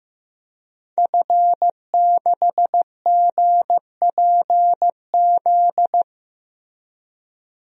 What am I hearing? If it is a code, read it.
Morse code, heard as a keyed tone, about 15 wpm, 700 Hz. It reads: F6GPZ
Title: DXCC-100 Call Signs - 5 Characters - Repeated 3x - 15wpm